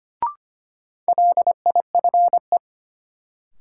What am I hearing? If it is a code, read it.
Morse code, heard as a keyed tone, about 25 wpm, 700 Hz. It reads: LIFE
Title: Sentences from Top 300 Words - New Word Drill - 25wpm